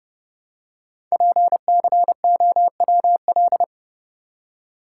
Morse code, heard as a keyed tone, about 30 wpm, 690 Hz. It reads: PCOWL
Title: Code Groups - 5 Letters - 30wpm